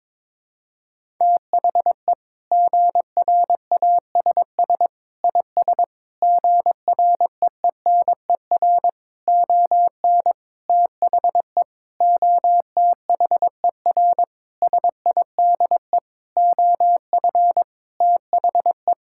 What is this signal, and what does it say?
Morse code, heard as a keyed tone, about 22 wpm, 700 Hz. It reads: THE GRASS IS GREENER ON THE OTHER SIDE OF THE
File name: practice / 22wpm / Sentences with 4-Word Phrases - 22wpm